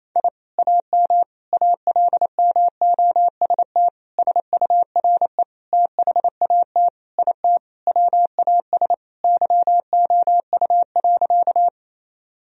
Morse code, heard as a keyed tone, about 28 wpm, 700 Hz. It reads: I AM ALMOST SURE THAT IT WAS YOU.
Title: Sentences from Top 1000 Words - Review Part 4 - 28wpm